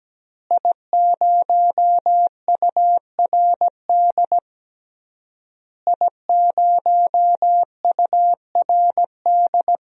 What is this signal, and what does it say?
Morse code, heard as a keyed tone, about 17 wpm, 700 Hz. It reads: I0URD I0URD
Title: DXCC-100 Call Signs - 5 Characters - Repeated 3x - 17wpm